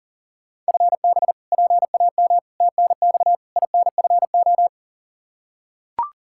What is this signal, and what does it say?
Morse code, heard as a keyed tone, about 37 wpm, 700 Hz. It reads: FB PAM TNX INFO
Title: Practice QSO - Exchange 3 - More Info - 40wpm